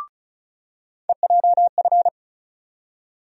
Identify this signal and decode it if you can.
Morse code, heard as a keyed tone, about 34 wpm, 700 Hz. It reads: EJF
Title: USA Call Sign Postfixes - 3 Letters 35wpm